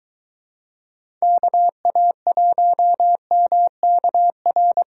Morse code, heard as a keyed tone, about 24 wpm, 700 Hz. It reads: KA1MKR